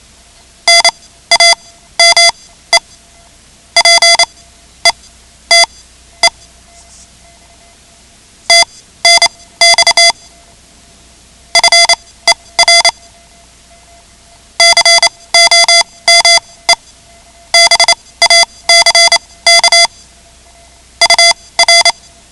Morse code, heard as a keyed tone, about 28 wpm, spaced as for 16 wpm, 2050 Hz. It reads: NAME PETE TNX FER COME BACK UR
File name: ham radio / CodeQuick / Code Quick 10